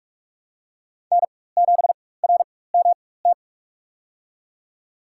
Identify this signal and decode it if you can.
Morse code, heard as a keyed tone, about 45 wpm, spaced as for 21 wpm, 700 Hz. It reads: N7RMT
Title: USA Call Signs - 5 Characters - 4x Character Spacing - 45wpm